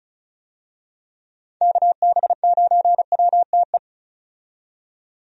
Morse code, heard as a keyed tone, about 36 wpm, 700 Hz. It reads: KB9WTE
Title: USA Call Signs - 6 Characters - 35wpm